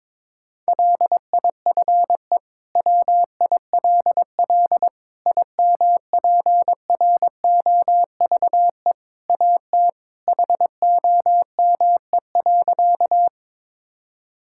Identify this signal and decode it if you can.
Morse code, heard as a keyed tone, about 22 wpm, 700 Hz. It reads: LIFE WILL IMPROVE AT HOME.
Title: Sentences from Top 1000 Words - Review Part 5 - 22wpm